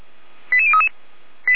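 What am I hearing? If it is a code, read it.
Five-tone selective call (ZVEI), heard as quick runs of five tones, tones of about 80 ms. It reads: 88033
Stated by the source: ZVEI five-tone selective call 88507 received by SDR.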